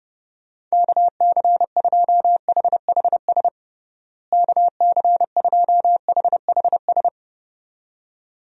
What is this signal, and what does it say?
Morse code, heard as a keyed tone, about 30 wpm, 700 Hz. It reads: KC2HHS KC2HHS
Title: USA Call Signs - 6 Characters - Repeated 2x - 30wpm